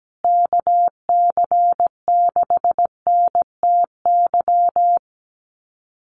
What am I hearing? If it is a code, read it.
Morse code, heard as a keyed tone, about 17 wpm, 700 Hz. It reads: KC6NTY
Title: USA Call Signs Repeated 2x 17wpm